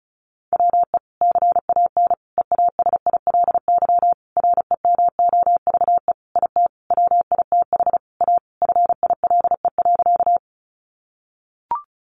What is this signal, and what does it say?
Morse code, heard as a keyed tone, about 34 wpm, 700 Hz. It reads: WE CAN EASILY REMOVE IT WITH A FILE.
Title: Sentences from Top 1200 Words 35wpm